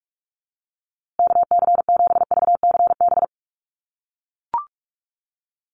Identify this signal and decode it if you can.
Morse code, heard as a keyed tone, about 44 wpm, 700 Hz. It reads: KC7VCB
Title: USA Call Signs Repeated 2x 45wpm